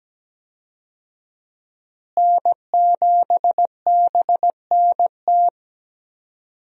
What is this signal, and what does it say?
Morse code, heard as a keyed tone, about 17 wpm, 700 Hz. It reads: N7BNT
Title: USA Call Signs - 5 Characters - Repeated 2x - 17wpm